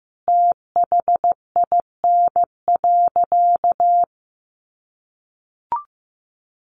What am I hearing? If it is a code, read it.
Morse code, heard as a keyed tone, about 15 wpm, 700 Hz. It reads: THIN.
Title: Sentences from Top 1200 Words 15wpm